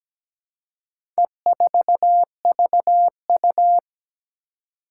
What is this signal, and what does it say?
Morse code, heard as a keyed tone, about 17 wpm, 700 Hz. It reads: E4VU
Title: Four Challenging Letter-Number Combinations 17wpm